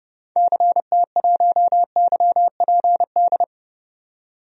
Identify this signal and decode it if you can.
Morse code, heard as a keyed tone, about 30 wpm, 700 Hz. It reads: CT1YPD
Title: DXCC-100 Call Signs - 6 Characters - 30wpm